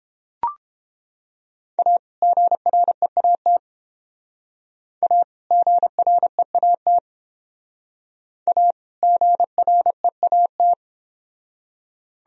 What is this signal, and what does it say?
Morse code, heard as a keyed tone, about 30 wpm, 700 Hz. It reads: A GREAT A GREAT A GREAT